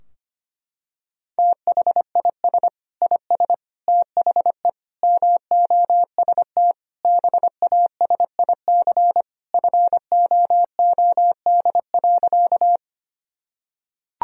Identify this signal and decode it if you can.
Morse code, heard as a keyed tone, about 25 wpm, 700 Hz. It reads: THIS IS THE MOST BASIC FOOD.
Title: Sentences from Top 1000 Words - Review Part 3 - 25wpm